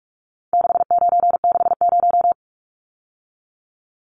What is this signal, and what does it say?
Morse code, heard as a keyed tone, about 44 wpm, 700 Hz. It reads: 6960
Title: Numbers - 4-Digit 45wpm